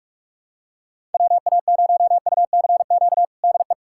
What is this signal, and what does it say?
Morse code, heard as a keyed tone, about 46 wpm, 690 Hz. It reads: WA0UCQ DE
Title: USA Callsign DE Callsign - 6 Characters - 45wpm